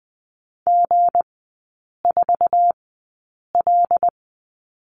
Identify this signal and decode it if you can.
Morse code, heard as a keyed tone, about 20 wpm, spaced as for 8 wpm, 690 Hz. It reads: G4L
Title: 166 Lesson 27e - Number 7 - Three Characters 20wpm